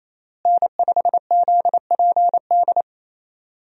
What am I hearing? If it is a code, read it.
Morse code, heard as a keyed tone, about 28 wpm, 700 Hz. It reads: N5ZPD
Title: USA Call Signs - 5 Characters - Repeated 2x - 28wpm